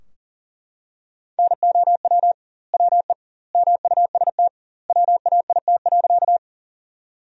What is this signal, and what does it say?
Morse code, heard as a keyed tone, about 41 wpm, 700 Hz. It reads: NOW WE MUST WAIT.
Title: 2x Word-Spacing - Sentences from Top 500 Words - 40wpm